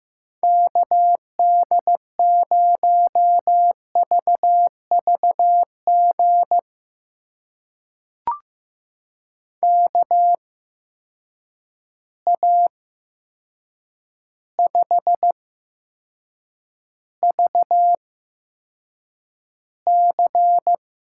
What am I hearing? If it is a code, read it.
Morse code, heard as a keyed tone, about 15 wpm, spaced as for 7 wpm, 700 Hz. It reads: KD0VVG KA5VC
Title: USA Call Signs - 6 Characters - 8x Character Spacing - 15wpm